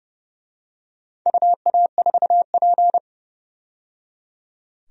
Morse code, heard as a keyed tone, about 30 wpm, 700 Hz. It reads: UA4P
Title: DXCC-100 Call Signs - 4 Characters - 30wpm